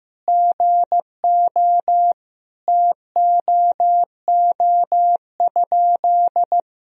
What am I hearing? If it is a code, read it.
Morse code, heard as a keyed tone, about 15 wpm, 700 Hz. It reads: GO TOO?
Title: Sentences from Top 1000 Words - Review Part 2 - 15wpm